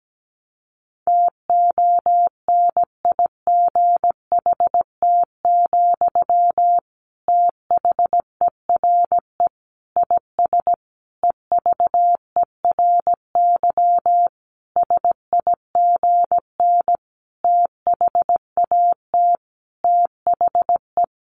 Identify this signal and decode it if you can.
Morse code, heard as a keyed tone, about 17 wpm, 700 Hz. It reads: TONIGHT, THERE IS EVERY SIGN THAT THE
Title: Sentences from Top 1400 Words 17wpm